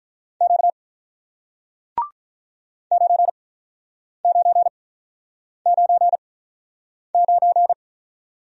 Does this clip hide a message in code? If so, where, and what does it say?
Yes: Morse code from 0.4 to 7.7 s, heard as a keyed tone, about 47 wpm, spaced as for 9 wpm, 700 Hz. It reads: Q 9999